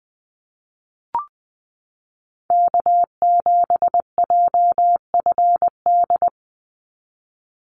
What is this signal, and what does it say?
Morse code, heard as a keyed tone, about 19 wpm, 690 Hz. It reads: K7JFD
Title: NA SOTA Chaser Callsigns 20wpm